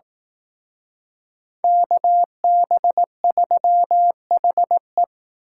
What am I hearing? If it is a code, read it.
Morse code, heard as a keyed tone, about 18 wpm, 700 Hz. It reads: KB3HE
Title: Speed Racing - USA 2x2 Call Signs 15wpm